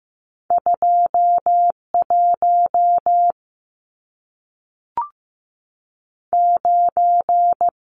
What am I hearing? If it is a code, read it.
Morse code, heard as a keyed tone, about 15 wpm, 700 Hz. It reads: 21 9
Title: Numbers - 2-Digit 15wpm